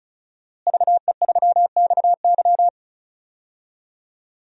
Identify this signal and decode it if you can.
Morse code, heard as a keyed tone, about 36 wpm, 700 Hz. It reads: VE3XY
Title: DXCC-100 Call Signs - 5 Characters - Repeated 3x - 35wpm